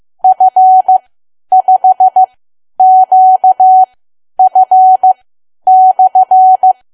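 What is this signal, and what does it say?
Morse code, heard as a keyed tone, about 15 wpm, spaced as for 10 wpm, 750 Hz. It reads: F5QF/
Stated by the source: Morse learning practice at 10 wpm.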